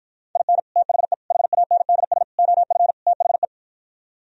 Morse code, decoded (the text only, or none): IN THE HANDS OF THE